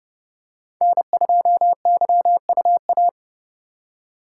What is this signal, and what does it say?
Morse code, heard as a keyed tone, about 30 wpm, 700 Hz. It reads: N2YUA
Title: USA Call Signs - 5 Characters - 3x Character Spacing - 30wpm